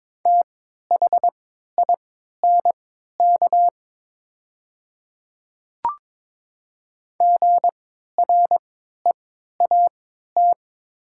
Morse code, heard as a keyed tone, about 23 wpm, spaced as for 13 wpm, 700 Hz. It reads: THINK GREAT
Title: 3x Character Spacing - Top 200 Words - 22wpm